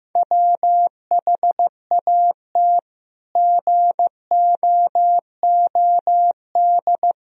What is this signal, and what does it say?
Morse code, heard as a keyed tone, about 15 wpm, 700 Hz. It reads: WHAT GOOD
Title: Sentences from Top 1000 Words - Review Part 3 - 15wpm